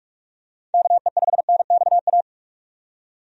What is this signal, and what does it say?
Morse code, heard as a keyed tone, about 45 wpm, 700 Hz. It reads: KE5NXA
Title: USA Call Signs - 6 Characters - 45wpm